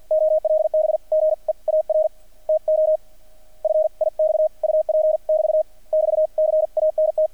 Morse code, heard as a keyed tone, about 50 wpm, spaced as for 43 wpm, 620 Hz. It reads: OLB GEAR TO FIX UP <BT> <BT> CANT
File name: archived / CRSnet